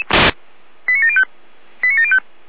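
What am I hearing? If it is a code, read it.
Five-tone selective call (ZVEI), heard as quick runs of five tones, tones of about 70 ms. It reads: 89705 89705
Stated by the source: ZVEI five-tone selective call received on an SDR.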